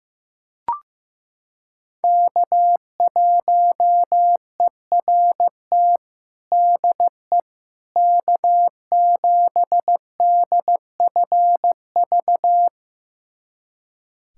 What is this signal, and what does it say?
Morse code, heard as a keyed tone, about 15 wpm, 700 Hz. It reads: K1ERT DE K7DFV
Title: USA Callsign DE Callsign - 5 Characters - 15wpm